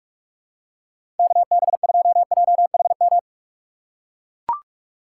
Morse code, heard as a keyed tone, about 44 wpm, 690 Hz. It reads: KB2JHM
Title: USA Call Signs - 4x Character Spacing - 45wpm